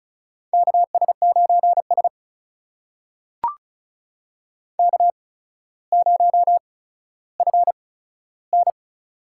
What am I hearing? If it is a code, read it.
Morse code, heard as a keyed tone, about 36 wpm, spaced as for 15 wpm, 700 Hz. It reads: KS9S K0FN
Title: USA Call Signs - 4 Characters - 8x Character Spacing - 35wpm